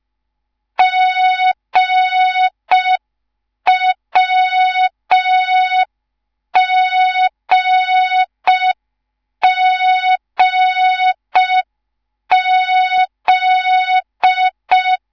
Morse code, heard as a keyed tone, about 5 wpm, 3700 Hz. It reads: GWGGZ